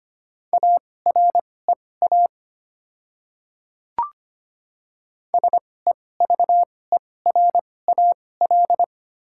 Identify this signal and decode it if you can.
Morse code, heard as a keyed tone, about 25 wpm, spaced as for 18 wpm, 700 Hz. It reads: AREA SEVERAL
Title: 2x Character Spacing - Top 500 Words - 25wpm